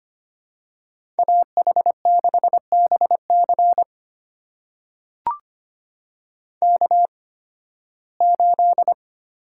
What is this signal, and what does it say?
Morse code, heard as a keyed tone, about 25 wpm, spaced as for 16 wpm, 700 Hz. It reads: AH6BC K8
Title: USA Call Signs - 5 Characters - 8x Character Spacing - 25wpm